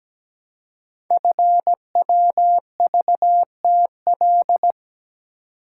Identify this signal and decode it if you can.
Morse code, heard as a keyed tone, about 17 wpm, 700 Hz. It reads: FWVTL
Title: Code Groups - 5 Letter:Numbers - 17wpm